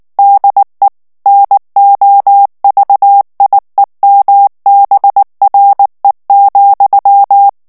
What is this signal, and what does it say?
Morse code, heard as a keyed tone, about 19 wpm, 800 Hz. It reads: DE NOVIEMBRE,
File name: morse mp3 / ESPAGNOL TEXTE EN CLAIR 2 VITESSE 18 MOTS MINUTES